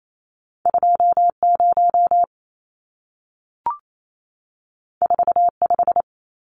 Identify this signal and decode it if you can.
Morse code, heard as a keyed tone, about 28 wpm, 700 Hz. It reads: 20 45
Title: Numbers - 2-Digit 28wpm